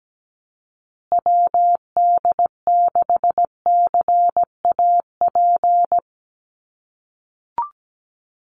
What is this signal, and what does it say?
Morse code, heard as a keyed tone, about 17 wpm, 700 Hz. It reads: WD6CAP
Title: USA Call Signs Repeated 2x 17wpm